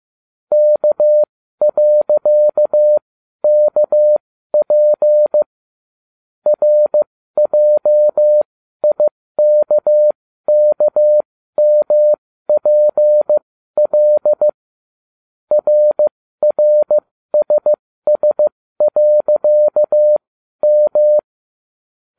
Morse code, heard as a keyed tone, about 15 wpm, spaced as for 12 wpm, 600 Hz. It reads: K.KP RJIKKMPL RRSS.M